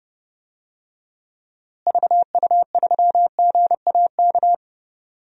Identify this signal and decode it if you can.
Morse code, heard as a keyed tone, about 32 wpm, 700 Hz. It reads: VU3GAK
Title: DXCC-100 Call Signs - 6 Characters - Repeated 2x - 30wpm